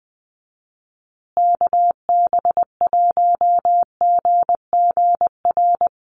Morse code, heard as a keyed tone, about 20 wpm, 700 Hz. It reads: KB1GGR
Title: USA Call Signs Repeated 3x 20wpm